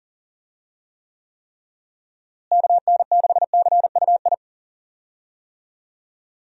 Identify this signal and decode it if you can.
Morse code, heard as a keyed tone, about 40 wpm, 700 Hz. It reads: KN6CUI
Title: USA Call Signs - 6 Characters - Repeated 3x - 40wpm